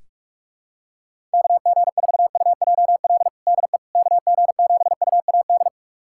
Morse code, heard as a keyed tone, about 41 wpm, 700 Hz. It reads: KG4UJL DE KG7UAD
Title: USA Callsign DE Callsign 45wpm